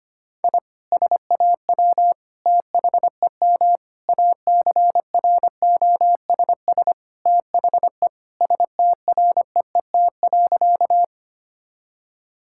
Morse code, heard as a keyed tone, about 25 wpm, 700 Hz. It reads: I SAW THEM ACROSS THE STREET.
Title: Sentences from Top 1000 Words - Review Part 5 - 25wpm